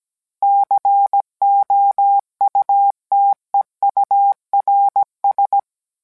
Morse code, heard as a keyed tone, about 17 wpm, 800 Hz. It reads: COUTEURS